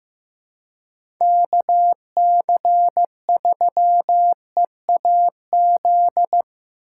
Morse code, heard as a keyed tone, about 15 wpm, 700 Hz. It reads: KC3EAZ
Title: USA Call Signs - 6 Characters - 3x Character Spacing - 15wpm